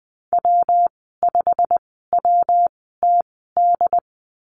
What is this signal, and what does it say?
Morse code, heard as a keyed tone, about 20 wpm, spaced as for 15 wpm, 700 Hz. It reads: W5WTD
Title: 193 Lesson 31g - Letter v - Call Signs 20wpm